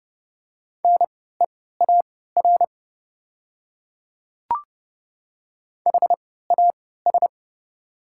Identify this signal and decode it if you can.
Morse code, heard as a keyed tone, about 30 wpm, spaced as for 17 wpm, 700 Hz. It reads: NEAR HAS
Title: 3x Character Spacing - Top 200 Words - 30wpm